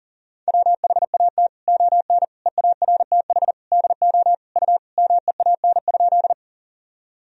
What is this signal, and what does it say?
Morse code, heard as a keyed tone, about 40 wpm, 700 Hz. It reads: WHAT ON EARTH DO U MEAN?
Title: Sentences from Top 200 Words and Abbreviations 40wpm